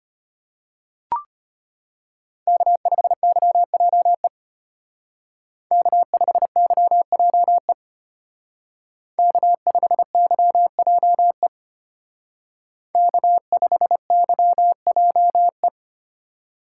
Morse code, heard as a keyed tone, about 32 wpm, 700 Hz. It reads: K5YJE K5YJE K5YJE K5YJE